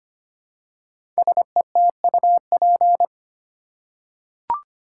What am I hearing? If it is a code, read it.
Morse code, heard as a keyed tone, about 25 wpm, 700 Hz. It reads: SETUP
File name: 5 Letter Words - First 3 Letters Given - 25wpm